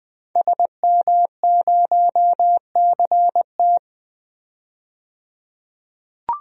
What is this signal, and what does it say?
Morse code, heard as a keyed tone, about 20 wpm, 700 Hz. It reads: SM0CT
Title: DXCC-100 Call Signs - 5 Characters - 20wpm